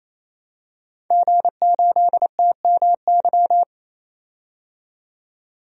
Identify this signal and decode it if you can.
Morse code, heard as a keyed tone, about 28 wpm, 700 Hz. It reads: G8TMY
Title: DXCC-100 Call Signs - 5 Characters - 28wpm